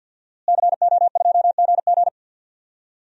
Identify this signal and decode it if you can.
Morse code, heard as a keyed tone, about 53 wpm, 700 Hz. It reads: CO2GG
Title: DXCC-100 Call Signs - 5 Characters - Repeated 3x - 50wpm